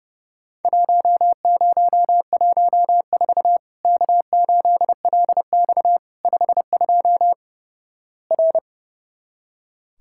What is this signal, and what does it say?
Morse code, heard as a keyed tone, about 29 wpm, 700 Hz. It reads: 1014 K8LX 52 R
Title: Worked All Europe DX Contest - QTC Exchange - 30wpm